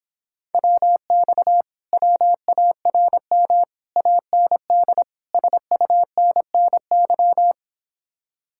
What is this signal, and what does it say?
Morse code, heard as a keyed tone, about 26 wpm, 700 Hz. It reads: WX WARM AND SUNNY